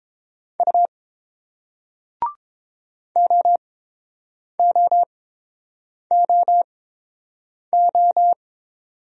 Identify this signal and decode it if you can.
Morse code, heard as a keyed tone, about 31 wpm, spaced as for 7 wpm, 700 Hz. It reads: U OOOO